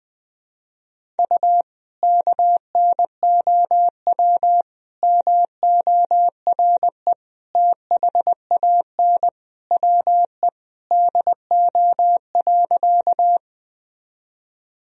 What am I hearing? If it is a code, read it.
Morse code, heard as a keyed tone, about 20 wpm, 700 Hz. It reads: U KNOW MORE THAN WE DO.
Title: Sentences from Top 100 Words and Abbreviations 20wpm